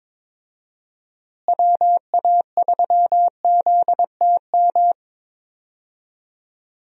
Morse code, heard as a keyed tone, about 22 wpm, 700 Hz. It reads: WA3ZTM
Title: USA Call Signs - 6 Characters - Repeated 2x - 22wpm